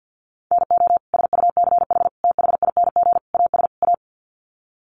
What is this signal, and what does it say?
Morse code, heard as a keyed tone, about 50 wpm, 700 Hz. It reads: NO SUCH THING AS A
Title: Sets of 5 Words Encore 50wpm